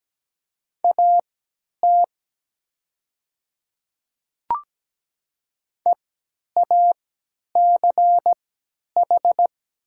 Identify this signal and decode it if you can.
Morse code, heard as a keyed tone, about 17 wpm, spaced as for 10 wpm, 700 Hz. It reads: AT EACH
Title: 3x Character Spacing - Top 100 Words - 17wpm